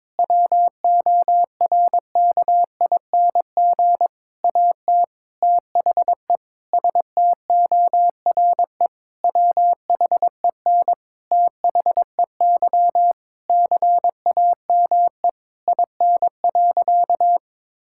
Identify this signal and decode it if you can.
Morse code, heard as a keyed tone, about 22 wpm, 700 Hz. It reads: WORKING AT THE STORE WHEN THEY CAME IN.
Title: Sentences from Top 1000 Words - Review Part 5 - 22wpm